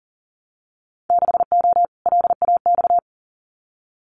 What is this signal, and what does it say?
Morse code, heard as a keyed tone, about 42 wpm, 700 Hz. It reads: 6O LAX
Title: Winter Field Day Exchanges Encore 40wpm